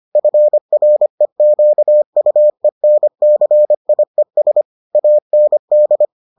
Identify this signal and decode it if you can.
Morse code, heard as a keyed tone, about 25 wpm, 600 Hz. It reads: FREQUENCIES AND